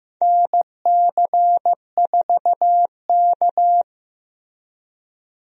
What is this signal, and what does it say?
Morse code, heard as a keyed tone, about 15 wpm, 700 Hz. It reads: NC4K
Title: USA Call Signs - 4 Characters - Repeated 2x - 15wpm